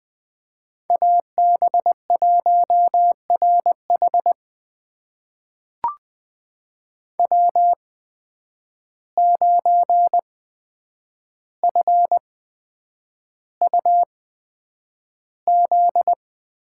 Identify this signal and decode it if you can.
Morse code, heard as a keyed tone, about 20 wpm, spaced as for 9 wpm, 700 Hz. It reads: AB1RH W9FUZ
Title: USA Call Signs - 5 Characters - 8x Character Spacing - 20wpm